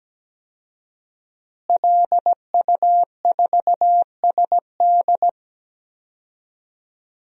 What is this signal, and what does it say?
Morse code, heard as a keyed tone, about 17 wpm, 700 Hz. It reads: LU4SD
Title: DXCC-100 Call Signs - 5 Characters - Repeated 2x - 17wpm